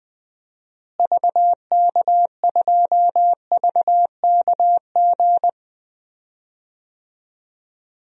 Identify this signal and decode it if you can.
Morse code, heard as a keyed tone, about 20 wpm, 700 Hz. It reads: VK2VKG
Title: DXCC-100 Call Signs - 6 Characters - Repeated 2x - 20wpm